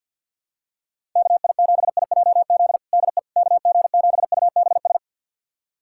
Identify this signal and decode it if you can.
Morse code, heard as a keyed tone, about 50 wpm, 700 Hz. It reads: KI7IJZ DE KG7UBS
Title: USA Callsign DE Callsign 50wpm